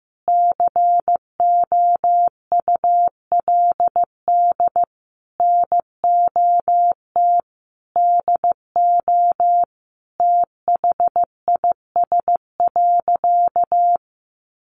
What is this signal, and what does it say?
Morse code, heard as a keyed tone, about 15 wpm, 700 Hz. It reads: COULD NOT DO THIS.
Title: Sentences from Top 1100 Words 15wpm